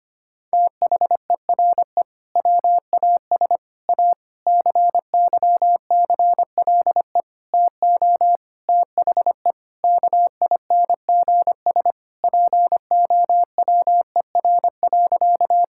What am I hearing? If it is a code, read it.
Morse code, heard as a keyed tone, about 25 wpm, 700 Hz. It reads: THERE WAS A CYCLE TO THE KINGS POWER.
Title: Sentences from Top 1000 Words - Review Part 6 - 25wpm